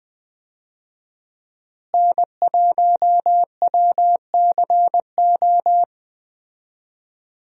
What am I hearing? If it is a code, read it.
Morse code, heard as a keyed tone, about 20 wpm, 700 Hz. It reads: N1WCO
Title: USA Call Signs - 5 Characters - Repeated 3x - 20wpm